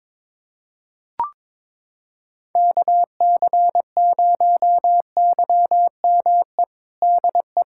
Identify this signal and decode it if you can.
Morse code, heard as a keyed tone, about 22 wpm, 700 Hz. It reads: KC0YME DE
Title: USA Callsign DE Callsign - 6 Characters - 22wpm